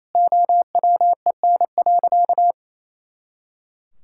Morse code, heard as a keyed tone, about 28 wpm, 700 Hz. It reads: OWEN.
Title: Pangram Challenge v2 - 28wpm